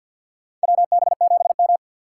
Morse code, heard as a keyed tone, about 50 wpm, 700 Hz. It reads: WB7M
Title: USA Call Signs - 4 Characters - 2x Character Spacing - 50wpm